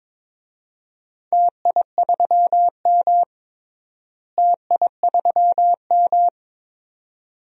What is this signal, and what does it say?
Morse code, heard as a keyed tone, about 22 wpm, 700 Hz. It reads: TI3M TI3M
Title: DXCC-100 Call Signs - 4 Characters - Repeated 2x - 22wpm